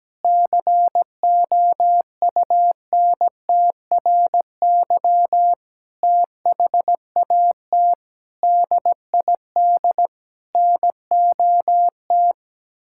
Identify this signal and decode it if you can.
Morse code, heard as a keyed tone, about 17 wpm, 700 Hz. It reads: COUNTRY THAT DID NOT
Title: Sentences from Top 1600 Words 17wpm